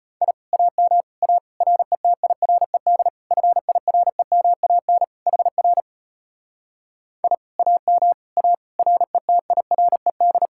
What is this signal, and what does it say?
Morse code, heard as a keyed tone, about 33 wpm, 700 Hz. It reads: I AM A RETIRED FIREMAN HR I AM A RETIRED